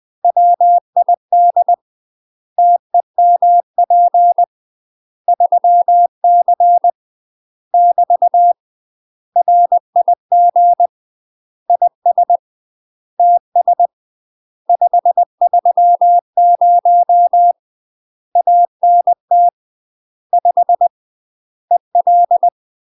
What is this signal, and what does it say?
Morse code, heard as a keyed tone, about 20 wpm, 700 Hz. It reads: WID TEMP 3C <BT> RIG IS TS 530 ANT 5 EL
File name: UK MASTER QSOs TRAINING / 20WPM / UK MASTER QSOs 0040